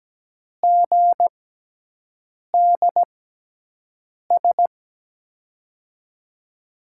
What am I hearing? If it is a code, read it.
Morse code, heard as a keyed tone, about 17 wpm, spaced as for 6 wpm, 700 Hz. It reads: GDS
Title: USA Call Signs - 6 Characters - 6x Character Spacing - 17wpm